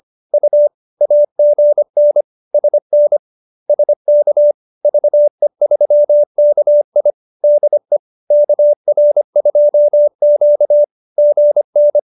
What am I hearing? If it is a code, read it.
Morse code, heard as a keyed tone, about 25 wpm, 570 Hz. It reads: U AGN SN SK VE3KI DE KR2Q GN